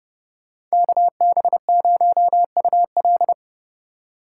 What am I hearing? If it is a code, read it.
Morse code, heard as a keyed tone, about 37 wpm, 700 Hz. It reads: KB0UL
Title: Speed Racing - USA 2x2 Call Signs 30wpm